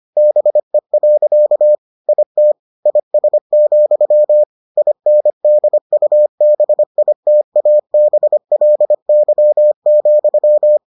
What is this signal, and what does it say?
Morse code, heard as a keyed tone, about 25 wpm, 600 Hz. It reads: BE. IT IS, INDUBITABLY,